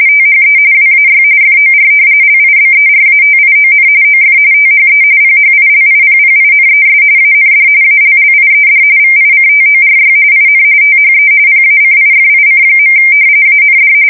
RTTY, about 45.45 baud. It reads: THE WAY FOR LIBERATION OF THE PHILIPPINES AND ALLIED VICTORY IN THE PACIFIC. AND
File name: fsk170 / battleshipnj